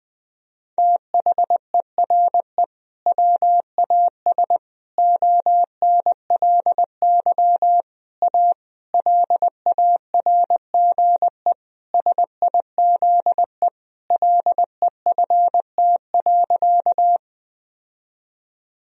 Morse code, heard as a keyed tone, about 21 wpm, 700 Hz. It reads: THERE WAS ONLY A LARGE SIZE LEFT.
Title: Sentences from Top 1000 Words - Review Part 2 - 20wpm